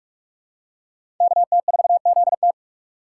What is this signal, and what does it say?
Morse code, heard as a keyed tone, about 46 wpm, 700 Hz. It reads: KT4ZT
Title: USA Call Signs - 5 Characters - 4x Character Spacing - 45wpm